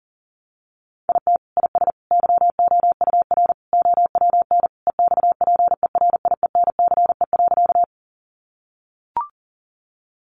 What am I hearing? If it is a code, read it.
Morse code, heard as a keyed tone, about 40 wpm, 700 Hz. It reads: IT IS YOUR OWN EXPERIENCE.
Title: Sentences from Top 600 Words 40wpm